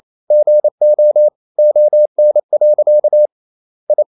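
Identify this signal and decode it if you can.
Morse code, heard as a keyed tone, about 28 wpm, 600 Hz. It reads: GO ON. I
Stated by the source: Morse code practice at 28 wpm, text of Ken Follett's novel Never in English, from CW med Gnister.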